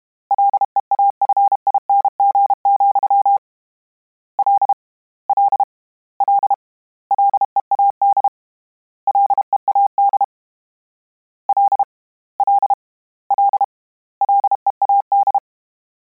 Morse code, heard as a keyed tone, about 32 wpm, 800 Hz. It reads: LEAFING, L L L LEAD LEAD L L L LEAD
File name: Combinations / Combinations-mp3s / Combinations@30